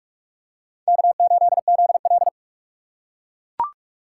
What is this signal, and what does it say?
Morse code, heard as a keyed tone, about 41 wpm, 700 Hz. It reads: K8ZL
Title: USA Call Signs - 3x Character Spacing - 45wpm